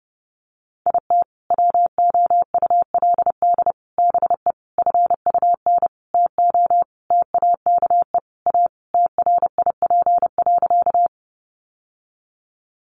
Morse code, heard as a keyed tone, about 30 wpm, 700 Hz. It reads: IT WOULD BE FUN TO TAKE A TRIP.
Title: Sentences from Top 1300 Words 30wpm